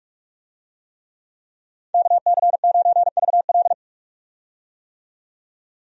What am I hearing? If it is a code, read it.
Morse code, heard as a keyed tone, about 46 wpm, 700 Hz. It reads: KC9VL
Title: USA Call Signs - 5 Characters - 45wpm